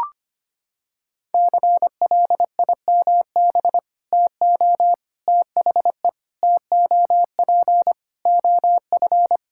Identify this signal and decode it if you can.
Morse code, heard as a keyed tone, about 25 wpm, 700 Hz. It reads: CLIMB TO THE TOP OF
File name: practice / 25wpm / Sentences with 4-Word Phrases - 25wpm